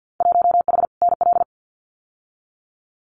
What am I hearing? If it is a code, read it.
Morse code, heard as a keyed tone, about 50 wpm, 700 Hz. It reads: 1H NL
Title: Winter Field Day Exchanges 50wpm